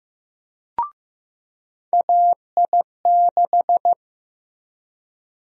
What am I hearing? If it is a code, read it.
Morse code, heard as a keyed tone, about 15 wpm, 700 Hz. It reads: AI6
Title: USA Call Sign Prefixes - 15wpm